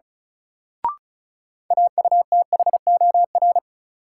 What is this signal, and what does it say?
Morse code, heard as a keyed tone, about 36 wpm, 700 Hz. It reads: AUTHOR
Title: Sentences from Top 1200 Words - New Word Drill - 35wpm